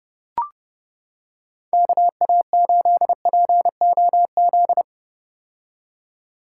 Morse code, heard as a keyed tone, about 30 wpm, 700 Hz. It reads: KA8POZ
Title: K1USN SST QSO Practice - 30wpm